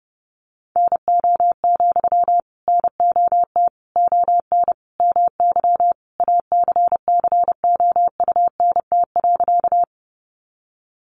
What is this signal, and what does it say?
Morse code, heard as a keyed tone, about 30 wpm, 700 Hz. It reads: NO, NOT ON MY ACCOUNT.
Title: Sentences from Top 900 Words 30wpm